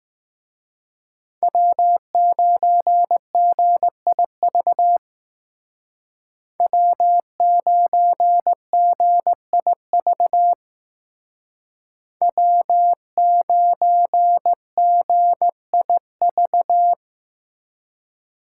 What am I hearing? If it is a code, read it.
Morse code, heard as a keyed tone, about 19 wpm, 700 Hz. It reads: W9GIV W9GIV W9GIV